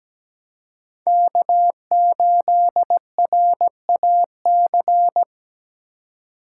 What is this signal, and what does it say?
Morse code, heard as a keyed tone, about 20 wpm, 700 Hz. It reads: K8RAC
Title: Speed Racing - USA 1x3 Call Signs 17wpm